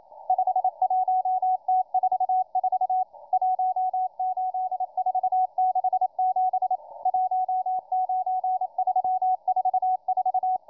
Morse code, heard as a keyed tone, about 27 wpm, 740 Hz, with radio noise. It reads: 51T44 18467 19344